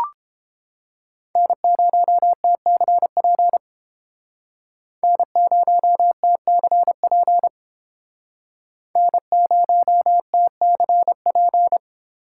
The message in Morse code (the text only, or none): N0TCP N0TCP N0TCP